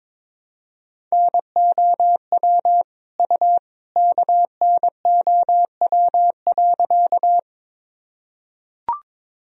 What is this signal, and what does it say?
Morse code, heard as a keyed tone, about 22 wpm, 700 Hz. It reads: NOW U KNOW.
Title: Sentences from Top 100 Words and Abbreviations 22wpm